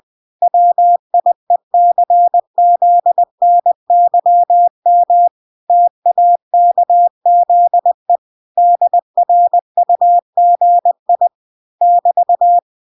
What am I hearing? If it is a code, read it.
Morse code, heard as a keyed tone, about 20 wpm, 700 Hz. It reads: WIECZNYM TAKZE DRUGI <BT>